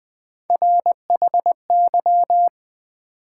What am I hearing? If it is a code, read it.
Morse code, heard as a keyed tone, about 20 wpm, 700 Hz. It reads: RHY
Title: Build A Word - 3 Letters At A Time - 6-Letter Words - 20wpm